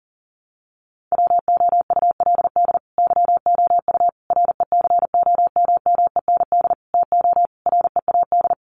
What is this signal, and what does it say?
Morse code, heard as a keyed tone, about 40 wpm, 700 Hz. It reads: WOULD YOU RECOMMEND TO READ
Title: Sentences from Top 1200 Words 40wpm